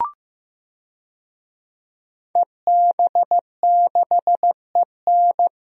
Morse code, heard as a keyed tone, about 15 wpm, 700 Hz. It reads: EB6EN
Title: DXCC-100 Call Signs - 5 Characters - Repeated 2x - 15wpm